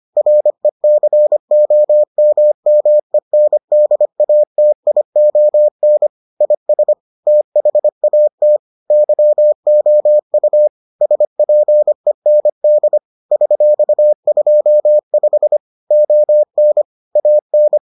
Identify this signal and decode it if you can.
Morse code, heard as a keyed tone, about 25 wpm, 600 Hz. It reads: RECOMMENDATION IS THAT YOU SPEND $25 ON AN